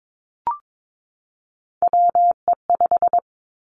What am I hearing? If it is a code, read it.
Morse code, heard as a keyed tone, about 22 wpm, 700 Hz. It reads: WE5
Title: Three Challenging Letter-Number Combinations 22wpm